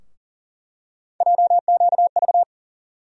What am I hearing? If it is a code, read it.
Morse code, heard as a keyed tone, about 40 wpm, 700 Hz. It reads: JQV
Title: USA Call Sign Postfixes - 3 Letters 40wpm